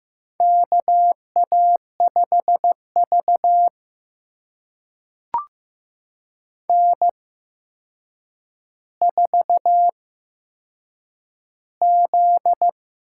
Morse code, heard as a keyed tone, about 15 wpm, spaced as for 7 wpm, 700 Hz. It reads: KA5V N4Z
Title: USA Call Signs - 4 Characters - 8x Character Spacing - 15wpm